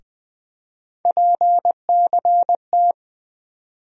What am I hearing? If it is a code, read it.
Morse code, heard as a keyed tone, about 20 wpm, 700 Hz. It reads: PCT
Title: USA Call Sign Postfixes - 3 Letters 20wpm